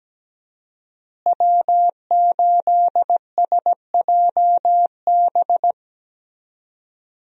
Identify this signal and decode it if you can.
Morse code, heard as a keyed tone, about 17 wpm, 700 Hz. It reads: W8SJB
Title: USA Call Signs - 5 Characters - Repeated 3x - 17wpm